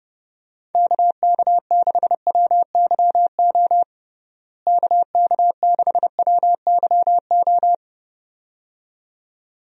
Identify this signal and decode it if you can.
Morse code, heard as a keyed tone, about 30 wpm, 700 Hz. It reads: KK6WYO KK6WYO
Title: USA Call Signs - 6 Characters - Repeated 2x - 30wpm